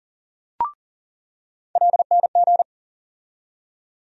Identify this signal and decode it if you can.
Morse code, heard as a keyed tone, about 40 wpm, 700 Hz. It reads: LNG
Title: USA Call Sign Postfixes - 3 Letters 40wpm